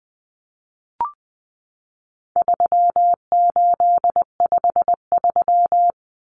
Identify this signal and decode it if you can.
Morse code, heard as a keyed tone, about 20 wpm, 700 Hz. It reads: 3853
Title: Common Cut Numbers - 4-Digit 20wpm